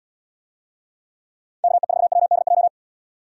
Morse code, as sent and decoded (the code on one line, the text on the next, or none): -.- . ....- .-- -.. .---
KE4WDJ